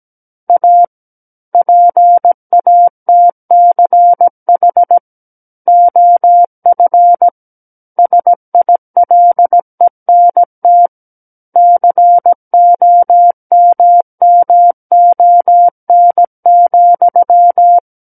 Morse code, heard as a keyed tone, about 17 wpm, 700 Hz. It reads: A PATCH OF SILENT COMMON,
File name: Book One / 08-Chapter 8 - 17 WPM